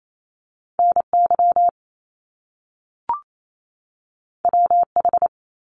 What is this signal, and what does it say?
Morse code, heard as a keyed tone, about 28 wpm, 700 Hz. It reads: N Y W H
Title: Top Two Letter Combinations in Words 28wpm